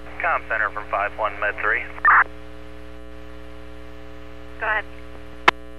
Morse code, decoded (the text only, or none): none